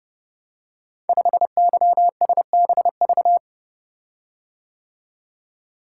Morse code, heard as a keyed tone, about 30 wpm, 700 Hz. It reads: 5YSBV
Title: Code Groups - 5 Letter:Numbers - 30wpm